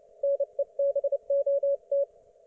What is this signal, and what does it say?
Morse code, heard as a keyed tone, about 29 wpm, 560 Hz. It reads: NEBOT